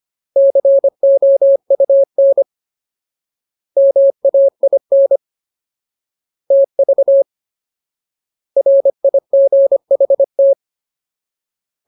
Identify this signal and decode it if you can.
Morse code, heard as a keyed tone, about 25 wpm, 550 Hz. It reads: COUN MAIN TV RIGHT
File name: Ultimate Word List / 25WPM Ultimate